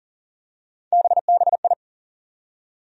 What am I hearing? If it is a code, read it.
Morse code, heard as a keyed tone, about 40 wpm, 700 Hz. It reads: BBI
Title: USA Call Sign Postfixes - 3 Letters 40wpm